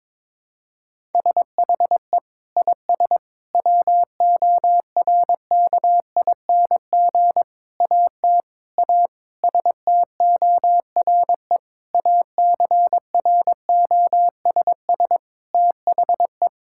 Morse code, heard as a keyed tone, about 22 wpm, 700 Hz. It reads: SHE IS WORKING AT A STORE ACROSS THE
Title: Sentences from Top 1000 Words - Review Part 5 - 22wpm